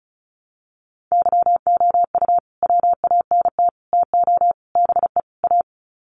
Morse code, heard as a keyed tone, about 34 wpm, 700 Hz. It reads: YOU WANT TO BE A
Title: Sets of 5 Words Encore2 35wpm